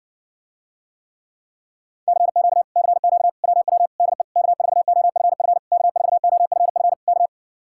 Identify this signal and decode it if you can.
Morse code, heard as a keyed tone, about 60 wpm, 700 Hz. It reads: CQ CQ WW DE K4OFF K4OFF K